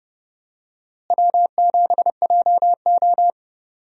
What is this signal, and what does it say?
Morse code, heard as a keyed tone, about 30 wpm, 700 Hz. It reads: W7JO
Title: USA Call Signs - 4 Characters - Repeated 2x - 30wpm